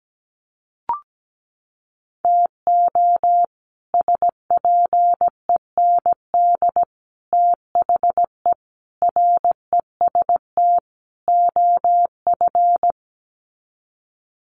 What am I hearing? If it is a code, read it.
Morse code, heard as a keyed tone, about 17 wpm, 700 Hz. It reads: TO SPEND THE REST OF
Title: Sets of 5 Words Encore2 17wpm